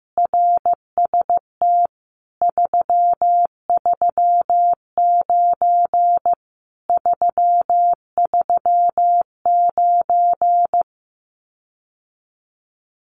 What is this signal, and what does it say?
Morse code, heard as a keyed tone, about 15 wpm, 700 Hz. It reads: RST 339 339
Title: Common QSO Phrases Encore 15wpm